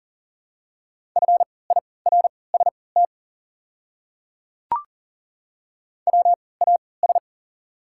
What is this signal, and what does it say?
Morse code, heard as a keyed tone, about 40 wpm, spaced as for 23 wpm, 690 Hz. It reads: FIRST WAS
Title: 3x Character Spacing - Top 100 Words - 40wpm